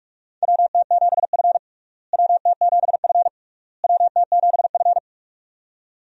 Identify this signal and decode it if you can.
Morse code, heard as a keyed tone, about 46 wpm, 700 Hz. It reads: WT7F WT7F WT7F
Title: USA Call Signs - 4 Characters - Repeated 3x - 45wpm